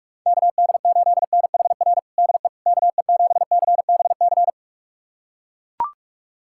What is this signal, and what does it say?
Morse code, heard as a keyed tone, about 41 wpm, 700 Hz. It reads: KD8NHR DE KE7CBC
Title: USA Callsign DE Callsign 45wpm